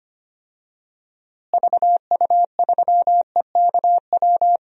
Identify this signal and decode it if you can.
Morse code, heard as a keyed tone, about 25 wpm, 700 Hz. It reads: VU3EKW
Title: DXCC-100 Call Signs - 6 Characters - Repeated 3x - 25wpm